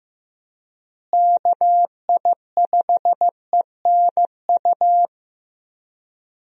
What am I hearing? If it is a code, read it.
Morse code, heard as a keyed tone, about 16 wpm, 700 Hz. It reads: KI5ENU
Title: USA Call Signs - 6 Characters - Repeated 2x - 15wpm